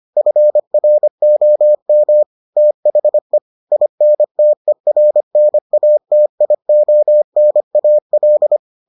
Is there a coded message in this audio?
Yes: Morse code from 0.2 to 8.6 s, heard as a keyed tone, about 25 wpm, 600 Hz. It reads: FROM THE INTERNATIONAL